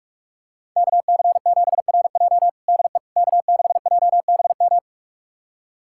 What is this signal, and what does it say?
Morse code, heard as a keyed tone, about 42 wpm, 700 Hz. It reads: KC7RJ DE K6JBM
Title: USA Callsign DE Callsign - 5 Characters - 45wpm